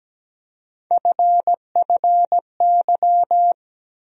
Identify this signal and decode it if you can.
Morse code, heard as a keyed tone, about 17 wpm, 700 Hz. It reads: FFY
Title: USA Call Sign Postfixes - 3 Letters 17wpm